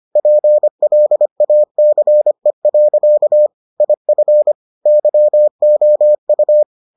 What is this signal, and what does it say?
Morse code, heard as a keyed tone, about 25 wpm, 600 Hz. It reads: PLACE. IF YOU